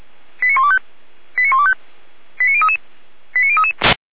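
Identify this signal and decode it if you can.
Five-tone selective call (ZVEI), heard as quick runs of five tones, tones of about 70 ms. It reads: 89125 89125 89033 89033
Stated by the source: ZVEI five-tone call 89070 received by SDR.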